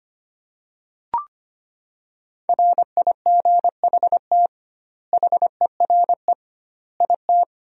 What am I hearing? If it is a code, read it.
Morse code, heard as a keyed tone, about 25 wpm, 700 Hz. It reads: RIGHT HERE IT
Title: 2x Word-Spacing - Sentences from Top 300 Words - 25wpm